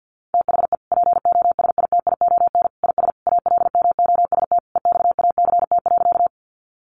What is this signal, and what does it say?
Morse code, heard as a keyed tone, about 50 wpm, 700 Hz. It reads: THE POSITION IS ALMOST EXACT.
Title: Sentences from Top 1300 Words 50wpm